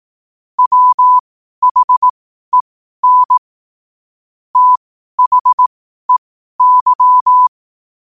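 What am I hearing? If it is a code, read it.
Morse code, heard as a keyed tone, about 18 wpm, spaced as for 12 wpm, 1000 Hz. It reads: WHEN THEY
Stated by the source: Morse code at 12 WPM.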